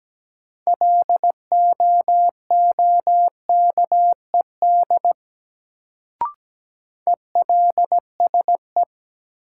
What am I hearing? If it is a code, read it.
Morse code, heard as a keyed tone, about 17 wpm, 700 Hz. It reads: LOOKED ELSE
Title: Sentences from Top 800 Words - New Word Drill - 17wpm